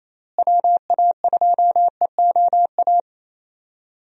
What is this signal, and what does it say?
Morse code, heard as a keyed tone, about 28 wpm, 700 Hz. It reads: WA2EOA
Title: USA Call Signs - 6 Characters - Repeated 3x - 28wpm